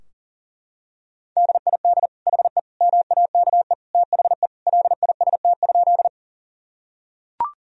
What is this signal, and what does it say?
Morse code, heard as a keyed tone, about 37 wpm, 700 Hz. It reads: DID HE MAKE THE LIST?
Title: Sentences from Top 1000 Words - Review Part 2 - 40wpm